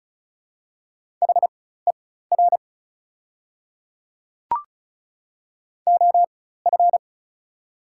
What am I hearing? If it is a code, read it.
Morse code, heard as a keyed tone, about 36 wpm, spaced as for 16 wpm, 700 Hz. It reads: HER OF
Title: 4x Character Spacing - Top 300 Words - 35wpm